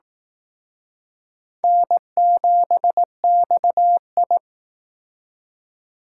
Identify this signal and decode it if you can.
Morse code, heard as a keyed tone, about 18 wpm, 700 Hz. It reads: N7XI
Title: Speed Racing - USA 1x2 Call Signs 15wpm